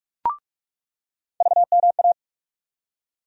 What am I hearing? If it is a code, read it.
Morse code, heard as a keyed tone, about 46 wpm, 700 Hz. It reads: VMA
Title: USA Call Sign Postfixes - 3 Letters 45wpm